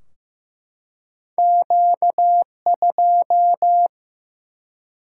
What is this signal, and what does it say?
Morse code, heard as a keyed tone, about 15 wpm, 700 Hz. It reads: Q2
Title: Single-Letter Single-Number 15wpm